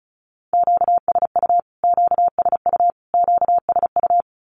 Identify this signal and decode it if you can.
Morse code, heard as a keyed tone, about 34 wpm, 700 Hz. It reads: QSU QSU QSU
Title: Q-Codes Repeated 3x 35wpm